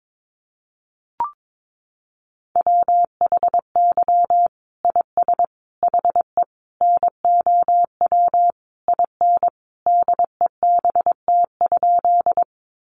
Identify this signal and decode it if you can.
Morse code, heard as a keyed tone, about 22 wpm, 700 Hz. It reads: WHY IS HE NOW IN DEBT?
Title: Sentences from Top 1300 Words 22wpm